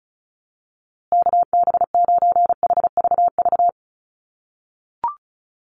Morse code, heard as a keyed tone, about 34 wpm, 700 Hz. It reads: KB9HVV
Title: USA Call Signs Repeated 3x 35wpm